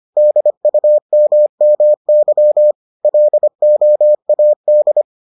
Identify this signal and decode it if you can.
Morse code, heard as a keyed tone, about 25 wpm, 600 Hz. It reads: DUMMY LOAD